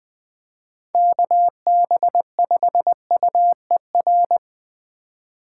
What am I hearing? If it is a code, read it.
Morse code, heard as a keyed tone, about 20 wpm, 700 Hz. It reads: KB5UER
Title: Speed Racing - USA 2x3 Call Signs 17wpm